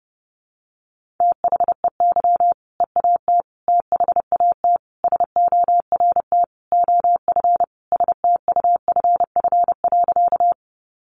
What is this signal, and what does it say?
Morse code, heard as a keyed tone, about 30 wpm, 700 Hz. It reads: THEY EAT THAT SORT OF STUFF.
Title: Sentences from Top 1400 Words 30wpm